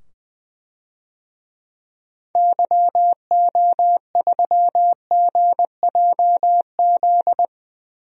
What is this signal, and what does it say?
Morse code, heard as a keyed tone, about 20 wpm, 700 Hz. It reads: YO3GJZ
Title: DXCC-100 Call Signs - 6 Characters - 20wpm